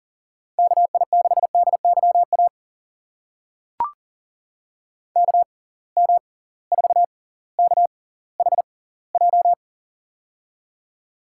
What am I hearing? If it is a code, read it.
Morse code, heard as a keyed tone, about 40 wpm, spaced as for 20 wpm, 700 Hz. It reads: KI6DYA KM4KHJ
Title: USA Call Signs - 6 Characters - 6x Character Spacing - 40wpm